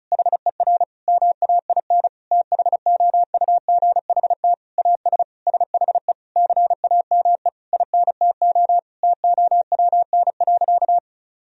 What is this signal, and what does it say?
Morse code, heard as a keyed tone, about 34 wpm, 700 Hz. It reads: HER MAIN THOUGHT AS SHE CAME INTO TOWN.
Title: Sentences from Top 1000 Words - Review Part 2 - 35wpm